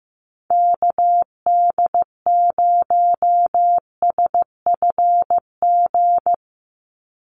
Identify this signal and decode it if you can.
Morse code, heard as a keyed tone, about 15 wpm, 700 Hz. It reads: KD0SFG
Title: USA Call Signs 15wpm